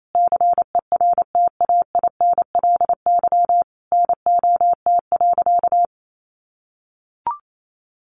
Morse code, heard as a keyed tone, about 28 wpm, 700 Hz. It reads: CERTAINLY NOT.
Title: Sentences from Top 900 Words 28wpm